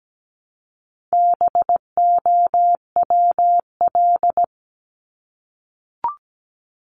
Morse code, heard as a keyed tone, about 17 wpm, 700 Hz. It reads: BOWL
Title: Top 1001-1250 Words 17wpm